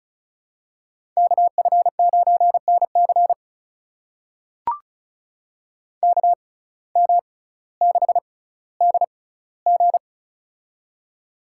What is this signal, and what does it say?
Morse code, heard as a keyed tone, about 36 wpm, spaced as for 18 wpm, 700 Hz. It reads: KF9NC KM6DG
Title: USA Call Signs - 5 Characters - 6x Character Spacing - 35wpm